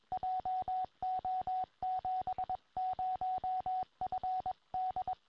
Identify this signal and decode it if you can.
Morse code, heard as a keyed tone, about 22 wpm, 740 Hz, with radio noise. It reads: JO70FD
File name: tropo / yl3iq